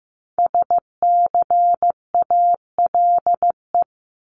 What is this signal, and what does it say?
Morse code, heard as a keyed tone, about 15 wpm, 700 Hz. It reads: SCALE
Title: Top 751-1000 Words 15wpm